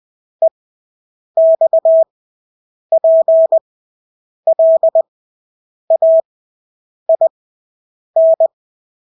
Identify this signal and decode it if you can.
Morse code, heard as a keyed tone, about 20 wpm, spaced as for 8 wpm, 650 Hz. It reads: EXPLAIN
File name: Engelske ord 20/8 wpm